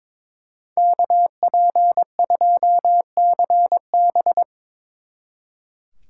Morse code, heard as a keyed tone, about 25 wpm, 700 Hz. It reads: KP2CB
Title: Speed Racing - USA 2x2 Call Signs 22wpm